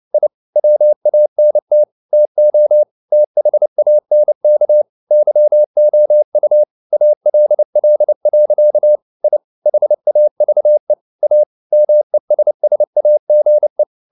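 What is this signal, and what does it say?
Morse code, heard as a keyed tone, about 29 wpm, 600 Hz. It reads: I WANT TO THANK YOU ALL. I HAVE A MESSAGE
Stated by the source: Morse code at 29 wpm, sending Ken Follett's novel Never in English, from CW med Gnister.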